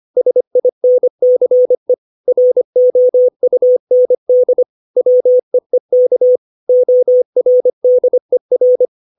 Morse code, heard as a keyed tone, about 25 wpm, 500 Hz. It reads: SINCE ROUND WEEK ORDER
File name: 632 words / 632 Words 25WPM